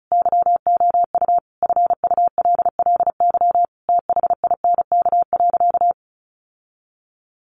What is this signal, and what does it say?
Morse code, heard as a keyed tone, about 34 wpm, 700 Hz. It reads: YOU FULLY THINK.
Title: Sentences from Top 1300 Words 35wpm